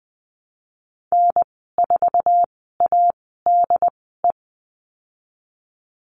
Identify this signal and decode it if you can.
Morse code, heard as a keyed tone, about 20 wpm, spaced as for 14 wpm, 700 Hz. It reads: N4ADE
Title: 088 Lesson 15g - Number 5 - Call Signs 20wpm